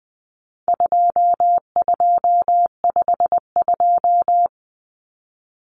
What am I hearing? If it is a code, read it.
Morse code, heard as a keyed tone, about 20 wpm, 700 Hz. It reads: 2252
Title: Common Cut Numbers - 4-Digit 20wpm